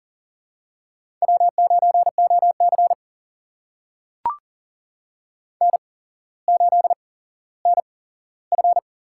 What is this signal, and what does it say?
Morse code, heard as a keyed tone, about 40 wpm, spaced as for 17 wpm, 700 Hz. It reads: W9OC N8NF
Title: USA Call Signs - 8x Character Spacing - 40wpm